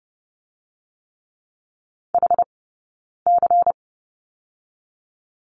Morse code, heard as a keyed tone, about 30 wpm, 700 Hz. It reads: H C
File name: 092 Lesson 16d - Letter c - Two Characters 30wpm